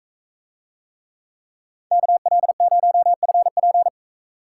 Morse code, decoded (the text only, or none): KL0FP